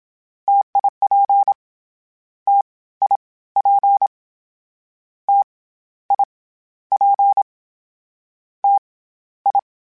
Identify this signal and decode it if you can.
Morse code, heard as a keyed tone, about 27 wpm, 800 Hz. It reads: TIP T I P T I P T I
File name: Variable Speed / Variable Speed-mp3s / T AOEI's sp @25